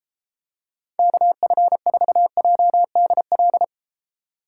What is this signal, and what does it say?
Morse code, heard as a keyed tone, about 33 wpm, 700 Hz. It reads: KF4JDL